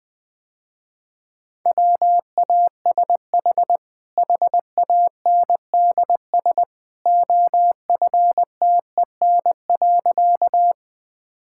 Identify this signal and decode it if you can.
Morse code, heard as a keyed tone, about 20 wpm, 700 Hz. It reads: WASH HANDS OFTEN.
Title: Easy Sentences with Hints - 20wpm